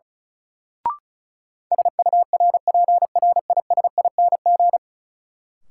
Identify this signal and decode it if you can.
Morse code, heard as a keyed tone, about 36 wpm, 700 Hz. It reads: SURPRISING
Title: Sets of 5 Words - New Word Drill - 35wpm